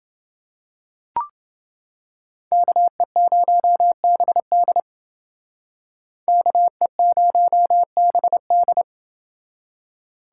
Morse code, heard as a keyed tone, about 27 wpm, 700 Hz. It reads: KE0BD KE0BD